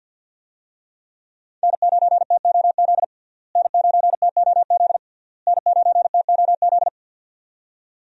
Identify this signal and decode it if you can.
Morse code, heard as a keyed tone, about 51 wpm, 700 Hz. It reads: N9TOZ N9TOZ N9TOZ
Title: USA Call Signs - 5 Characters - Repeated 3x - 50wpm